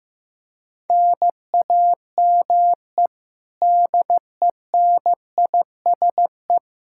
Morse code, heard as a keyed tone, about 15 wpm, 700 Hz. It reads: NAME DENISE
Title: Practice QSO - Exchange 1 - Beginning - 15wpm